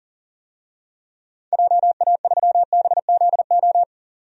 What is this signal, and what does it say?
Morse code, heard as a keyed tone, about 40 wpm, 700 Hz. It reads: JA3BZO
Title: DXCC-100 Call Signs - 6 Characters - 40wpm